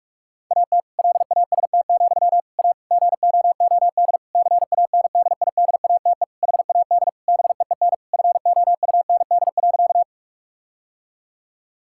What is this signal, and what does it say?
Morse code, heard as a keyed tone, about 46 wpm, 700 Hz. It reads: AT LAST, A GOOD CANDIDATE HAD BEEN FOUND.
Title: Sentences from Top 1600 Words 45wpm